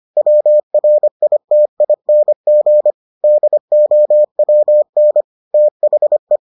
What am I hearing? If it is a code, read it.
Morse code, heard as a keyed tone, about 25 wpm, 600 Hz. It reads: WRITING DOWN THE